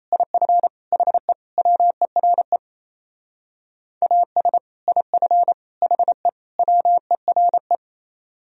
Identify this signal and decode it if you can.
Morse code, heard as a keyed tone, about 37 wpm, 700 Hz. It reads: IF HE WERE AS IF HE WERE